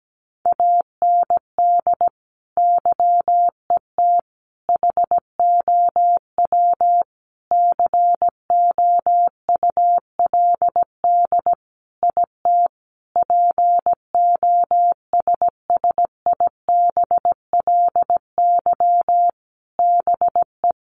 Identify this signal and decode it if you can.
Morse code, heard as a keyed tone, about 17 wpm, 700 Hz. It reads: AND YET HOW COULD IT POSSIBLY BE
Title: Sentences from Top 1300 Words 17wpm